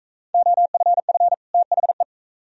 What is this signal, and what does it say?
Morse code, heard as a keyed tone, about 41 wpm, 700 Hz. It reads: OFF THE